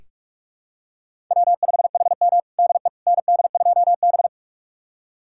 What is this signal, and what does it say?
Morse code, heard as a keyed tone, about 46 wpm, 700 Hz. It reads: W5HM DE ND2B
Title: USA Callsign DE Callsign - 4 Characters - 45wpm